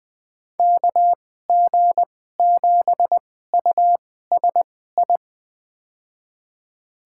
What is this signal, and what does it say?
Morse code, heard as a keyed tone, about 20 wpm, spaced as for 15 wpm, 700 Hz. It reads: KG7USI
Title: USA Call Signs - 6 Characters - 2x Character Spacing - 20wpm